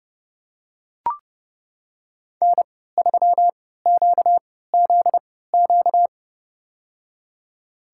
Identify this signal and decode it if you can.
Morse code, heard as a keyed tone, about 31 wpm, spaced as for 17 wpm, 700 Hz. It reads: N3QZQ
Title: USA Call Signs - 5 Characters - 3x Character Spacing - 30wpm